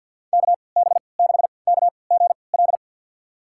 Morse code, heard as a keyed tone, about 53 wpm, spaced as for 29 wpm, 700 Hz. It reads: KB6KGL
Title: USA Call Signs - 6 Characters - 3x Character Spacing - 50wpm